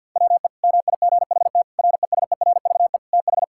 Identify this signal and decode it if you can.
Morse code, heard as a keyed tone, about 50 wpm, 700 Hz. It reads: WE MIGHT RESERVE TH
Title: Sentences from Top 1000 Words - Review Part 6 - 50wpm